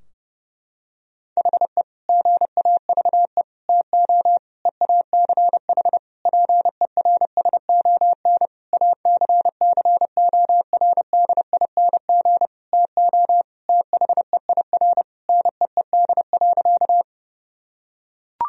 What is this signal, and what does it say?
Morse code, heard as a keyed tone, about 30 wpm, 700 Hz. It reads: HE GAVE TO EACH PERSON ACCORDING TO THEIR NEED.
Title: Sentences from Top 1000 Words - Review Part 4 - 30wpm